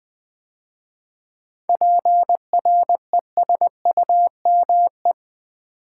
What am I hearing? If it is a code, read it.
Morse code, heard as a keyed tone, about 20 wpm, 700 Hz. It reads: PRESUME
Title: 7 Letter Words - First 3 Letters Given - 20wpm